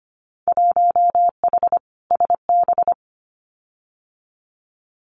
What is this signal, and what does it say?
Morse code, heard as a keyed tone, about 25 wpm, 700 Hz. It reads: 1H SB
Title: Winter Field Day Exchanges 25wpm